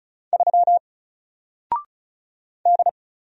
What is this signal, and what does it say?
Morse code, heard as a keyed tone, about 36 wpm, 700 Hz. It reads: 3 D
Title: Single Character (Letter:Number:Punctuation:Prosign) 35wpm